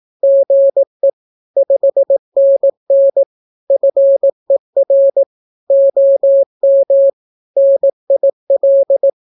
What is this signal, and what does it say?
Morse code, heard as a keyed tone, about 18 wpm, 550 Hz. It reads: GE 5NN FER OM NIL
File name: naukapodstaw / skroty / skroty